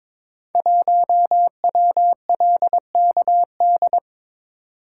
Morse code, heard as a keyed tone, about 22 wpm, 700 Hz. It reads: 1WLKD
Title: Code Groups - 5 Letter:Numbers - 22wpm